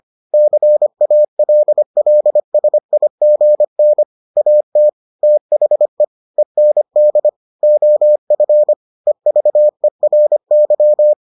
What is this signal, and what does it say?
Morse code, heard as a keyed tone, about 25 wpm, 600 Hz. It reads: CALLSIGN AT THE END OF EVERY